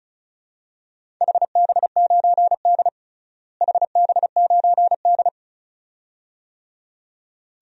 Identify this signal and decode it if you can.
Morse code, heard as a keyed tone, about 36 wpm, 700 Hz. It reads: HB9D HB9D
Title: DXCC-100 Call Signs - 4 Characters - Repeated 2x - 35wpm